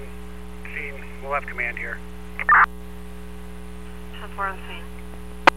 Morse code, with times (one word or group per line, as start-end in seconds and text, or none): none